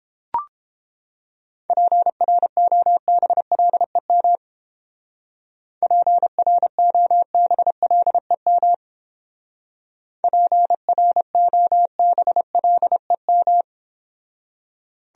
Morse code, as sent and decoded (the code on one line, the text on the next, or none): .--. .-. --- -... .-.. -- / .--. .-. --- -... .-.. -- / .--. .-. --- -... .-.. --
PROBLEM PROBLEM PROBLEM